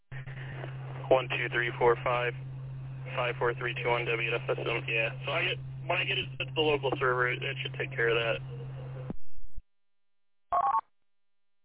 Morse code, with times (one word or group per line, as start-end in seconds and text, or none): none